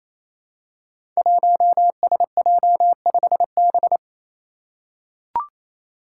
Morse code, heard as a keyed tone, about 27 wpm, 700 Hz. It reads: 1SJ5B
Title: Five Challenging Letter-Number Combinations 28wpm